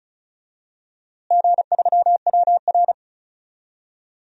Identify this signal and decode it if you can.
Morse code, heard as a keyed tone, about 36 wpm, 700 Hz. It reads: G3WR
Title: DXCC-100 Call Signs - 4 Characters - Repeated 3x - 35wpm